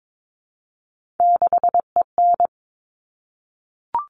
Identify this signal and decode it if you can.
Morse code, heard as a keyed tone, about 22 wpm, 700 Hz. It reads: 6EN
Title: All Cut Numbers - 3-Digit 22wpm